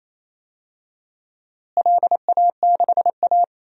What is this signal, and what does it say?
Morse code, heard as a keyed tone, about 28 wpm, 700 Hz. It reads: LA6A
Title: DXCC-100 Call Signs - 4 Characters - Repeated 3x - 28wpm